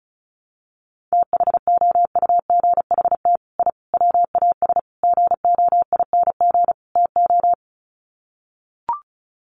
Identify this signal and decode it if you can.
Morse code, heard as a keyed tone, about 34 wpm, 700 Hz. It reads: THOUGHT I WAS GOING TO
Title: Sets of 5 Words Encore2 35wpm